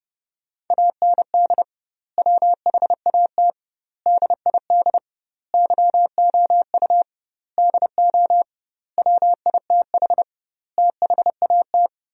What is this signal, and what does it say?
Morse code, heard as a keyed tone, about 30 wpm, 700 Hz. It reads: AND WHAT DID YOU DO WITH THAT
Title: 2x Word-Spacing - Sentences from Top 300 Words - 30wpm